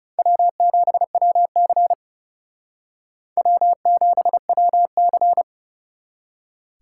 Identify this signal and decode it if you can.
Morse code, heard as a keyed tone, about 38 wpm, 700 Hz. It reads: W7WC W7WC